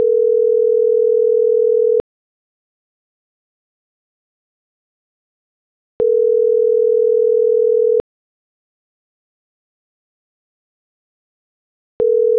Ringback tone (North America), 440 + 480 Hz.